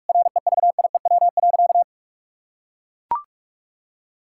Morse code, heard as a keyed tone, about 41 wpm, 700 Hz. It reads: REVIEW.
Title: Sentences from Top 1000 Words - Review Part 6 - 45wpm